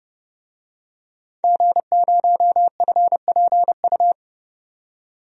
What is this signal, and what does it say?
Morse code, heard as a keyed tone, about 30 wpm, 700 Hz. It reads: G0FPU
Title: DXCC-100 Call Signs - 5 Characters - 30wpm